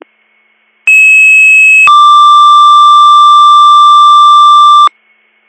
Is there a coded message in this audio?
Yes: two-tone page from 0.9 to 4.9 s, heard as a one-second tone, then a three-second tone, about 2600 then 1180 Hz, with noise.